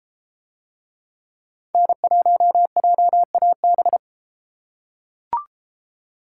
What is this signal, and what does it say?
Morse code, heard as a keyed tone, about 30 wpm, 700 Hz. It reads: N1JAB